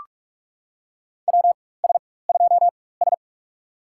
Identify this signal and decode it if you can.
Morse code, heard as a keyed tone, about 46 wpm, spaced as for 21 wpm, 690 Hz. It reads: WS2S
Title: USA Call Signs - 4x Character Spacing - 45wpm